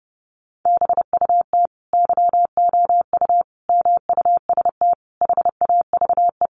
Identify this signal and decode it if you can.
Morse code, heard as a keyed tone, about 30 wpm, 700 Hz. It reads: BUT YOU MUST HAVE
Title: Sentences from Top 1300 Words 30wpm